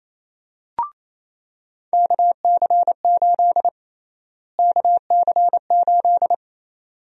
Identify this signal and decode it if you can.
Morse code, heard as a keyed tone, about 28 wpm, 700 Hz. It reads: KC8 KC8
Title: USA Call Sign Prefixes - Repeated 2x - 28wpm